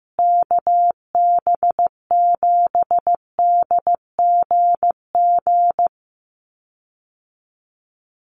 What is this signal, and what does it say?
Morse code, heard as a keyed tone, about 15 wpm, 700 Hz. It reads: KB7DGG
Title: USA Call Signs Repeated 2x 15wpm